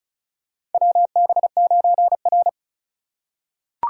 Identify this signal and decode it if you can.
Morse code, heard as a keyed tone, about 36 wpm, 700 Hz. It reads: WB9R
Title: USA Call Signs - 4 Characters - 4x Character Spacing - 35wpm